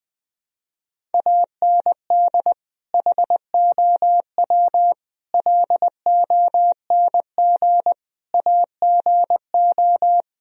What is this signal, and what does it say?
Morse code, heard as a keyed tone, about 20 wpm, 700 Hz. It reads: AND HOW LONG AGO
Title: Sentences from Top 1000 Words - Review Part 3 - 20wpm